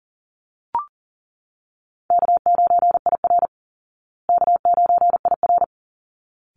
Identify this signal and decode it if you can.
Morse code, heard as a keyed tone, about 37 wpm, 700 Hz. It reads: K9IR K9IR
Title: Easy Call Signs Repeated 2x - 40wpm